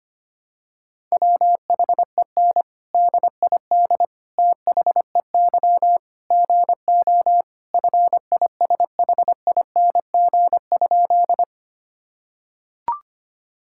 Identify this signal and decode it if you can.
Morse code, heard as a keyed tone, about 25 wpm, 700 Hz. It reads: WHEN DID THEY GO FISHING?
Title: Sentences from Top 1000 Words - Review Part 5 - 25wpm